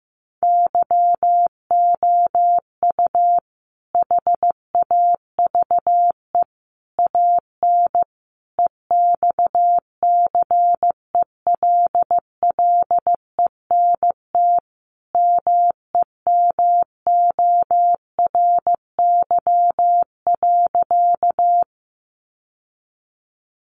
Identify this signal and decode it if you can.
Morse code, heard as a keyed tone, about 15 wpm, 700 Hz. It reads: YOU HAVE AN EXCELLENT MEMORY.
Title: Sentences from Top 1300 Words 15wpm